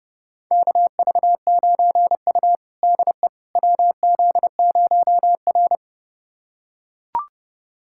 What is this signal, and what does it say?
Morse code, heard as a keyed tone, about 30 wpm, 700 Hz. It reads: KV9U DE WZ0R
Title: USA Callsign DE Callsign - 4 Characters - 30wpm